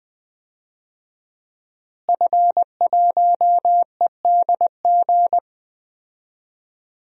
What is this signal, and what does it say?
Morse code, heard as a keyed tone, about 20 wpm, 700 Hz. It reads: F1EDG
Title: DXCC-100 Call Signs - 5 Characters - Repeated 2x - 20wpm